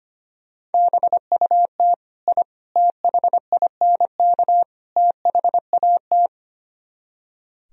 Morse code, heard as a keyed tone, about 31 wpm, 700 Hz. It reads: BUT I THINK THAT